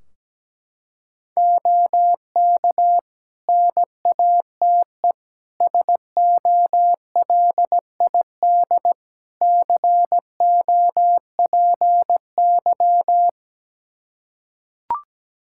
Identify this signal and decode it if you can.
Morse code, heard as a keyed tone, about 17 wpm, 700 Hz. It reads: OK NATE SOLID COPY
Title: Practice QSO - Exchange 3 - More Info - 17wpm